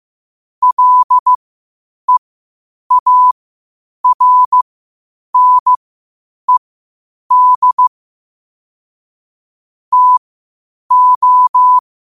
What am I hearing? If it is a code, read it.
Morse code, heard as a keyed tone, about 15 wpm, spaced as for 8 wpm, 1000 Hz. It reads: LEARNED TO
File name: qotdfiles / 8wpm